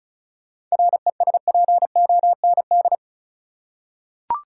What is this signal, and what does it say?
Morse code, heard as a keyed tone, about 34 wpm, 700 Hz. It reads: RESPOND
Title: Seven Letter Words - 35wpm